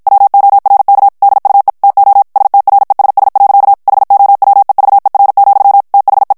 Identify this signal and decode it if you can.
Morse code, heard as a keyed tone, about 56 wpm, 800 Hz. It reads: WORK DUE TO STRESS. HOWEVER, THE
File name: morse mp3 / ANGLAIS texte clair 1 vitesse 50 mots minute